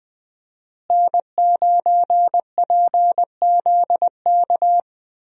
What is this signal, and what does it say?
Morse code, heard as a keyed tone, about 20 wpm, 700 Hz. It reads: N9PZK
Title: USA Call Signs - 5 Characters - Repeated 3x - 20wpm